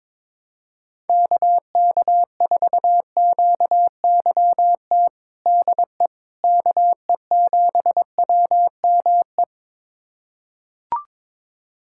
Morse code, heard as a keyed tone, about 22 wpm, 700 Hz. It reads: KK4QYT DE KE7WME
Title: USA Callsign DE Callsign - 6 Characters - 22wpm